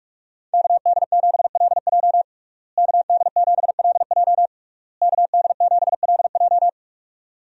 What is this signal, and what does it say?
Morse code, heard as a keyed tone, about 41 wpm, 700 Hz. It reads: KD7LJ KD7LJ KD7LJ
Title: USA Call Signs - 5 Characters - Repeated 3x - 45wpm